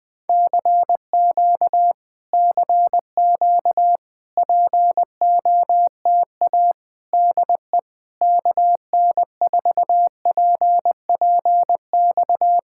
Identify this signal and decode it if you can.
Morse code, heard as a keyed tone, about 20 wpm, 700 Hz. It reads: CQ CQ POTA DE KN4PPX
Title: Speed-Racing Common QSO Phrases Encore - 20wpm